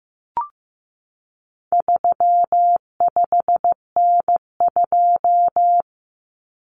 Morse code, heard as a keyed tone, about 15 wpm, 700 Hz. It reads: 35N2
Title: Common Cut Numbers - 4-Digit 15wpm